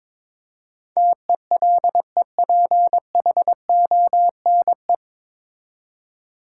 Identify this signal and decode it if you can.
Morse code, heard as a keyed tone, about 22 wpm, 700 Hz. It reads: TELEPHONE
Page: Morse Code Ninja practice 22wpm